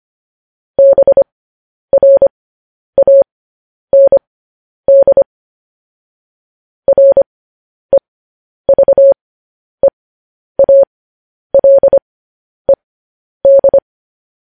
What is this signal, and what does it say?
Morse code, heard as a keyed tone, about 25 wpm, spaced as for 10 wpm, 550 Hz. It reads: BRAND REVEALED
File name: podcast / News headlines-1x-25x10